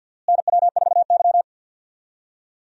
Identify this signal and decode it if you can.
Morse code, heard as a keyed tone, about 47 wpm, 690 Hz. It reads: NW4Y
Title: USA Call Signs - 4 Characters - Repeated 2x - 50wpm